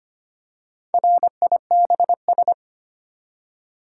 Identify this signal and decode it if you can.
Morse code, heard as a keyed tone, about 25 wpm, 700 Hz. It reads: RIBS
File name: practice / 25wpm / Easy Words with Hints - 25wpm